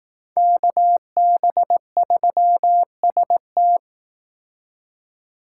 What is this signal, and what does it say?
Morse code, heard as a keyed tone, about 19 wpm, 700 Hz. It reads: KB3ST
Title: Speed Racing - USA 2x2 Call Signs 15wpm